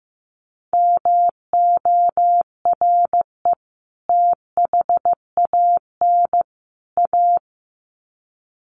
Morse code, heard as a keyed tone, about 15 wpm, 700 Hz. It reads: MORE THAN A
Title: Sets of 3 Words 15wpm